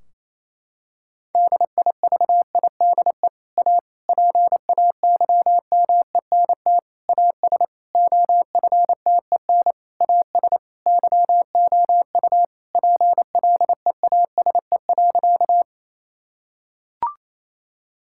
Morse code, heard as a keyed tone, about 27 wpm, 700 Hz. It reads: DIVIDE A PAYMENT AS OFTEN AS YOU PLEASE.
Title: Sentences from Top 1600 Words 28wpm